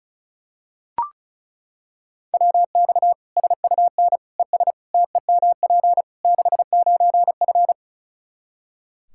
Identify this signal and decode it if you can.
Morse code, heard as a keyed tone, about 36 wpm, 700 Hz. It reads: WX SUN ES TEMP 69F
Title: K1USN SST QSO Practice 35wpm